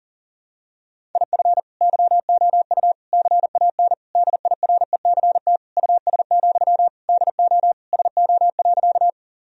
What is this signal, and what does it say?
Morse code, heard as a keyed tone, about 40 wpm, 700 Hz. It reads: IF YOU CAN DIRECT US, DO SO.
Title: Sentences from Top 1000 Words - Review Part 2 - 40wpm